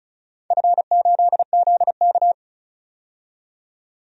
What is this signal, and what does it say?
Morse code, heard as a keyed tone, about 34 wpm, 700 Hz. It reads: F8ZK
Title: DXCC-100 Call Signs - 4 Characters - 35wpm